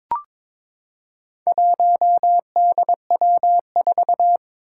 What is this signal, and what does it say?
Morse code, heard as a keyed tone, about 22 wpm, 700 Hz. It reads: 1DW4
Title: Four Challenging Letter-Number Combinations 22wpm